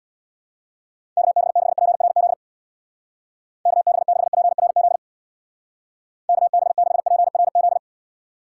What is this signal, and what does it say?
Morse code, heard as a keyed tone, about 65 wpm, 700 Hz. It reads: KB6PRZ KB6PRZ KB6PRZ